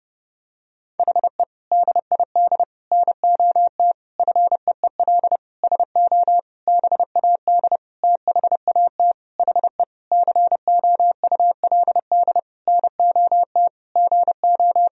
Morse code, heard as a keyed tone, about 30 wpm, 700 Hz. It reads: HE DID NOT FEEL SO BAD THAT HE COULD NOT GO
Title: Sentences from Top 1000 Words - Review Part 4 - 30wpm